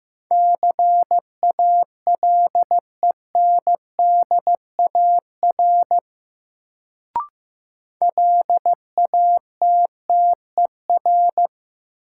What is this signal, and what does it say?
Morse code, heard as a keyed tone, about 15 wpm, 700 Hz. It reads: CALENDAR LATTER
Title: Sentences from Top 1600 Words - New Word Drill - 15wpm